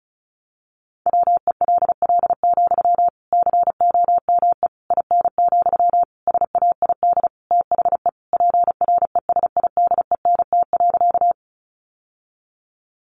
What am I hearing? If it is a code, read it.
Morse code, heard as a keyed tone, about 34 wpm, 700 Hz. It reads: WELL, COME IN, SAID THE PRESIDENT.
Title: Sentences from Top 1000 Words 35wpm